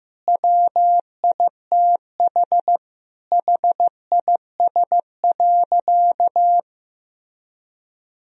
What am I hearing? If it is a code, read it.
Morse code, heard as a keyed tone, about 15 wpm, 700 Hz. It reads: WITH HIS.
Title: Sentences from Top 1000 Words - Review Part 3 - 15wpm